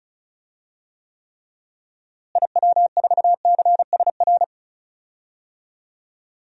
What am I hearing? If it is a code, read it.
Morse code, heard as a keyed tone, about 36 wpm, 700 Hz. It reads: IW4CSR
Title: DXCC-100 Call Signs - 6 Characters - Repeated 3x - 35wpm